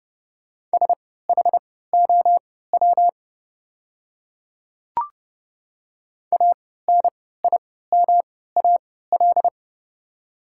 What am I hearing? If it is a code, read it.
Morse code, heard as a keyed tone, about 31 wpm, spaced as for 17 wpm, 700 Hz. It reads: SHOW ANIMAL
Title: 3x Character Spacing - Top 200 Words - 30wpm